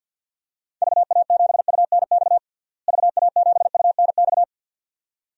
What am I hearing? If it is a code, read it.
Morse code, heard as a keyed tone, about 52 wpm, 700 Hz. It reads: VA7UNX VA7UNX
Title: DXCC-100 Call Signs - 6 Characters - Repeated 2x - 50wpm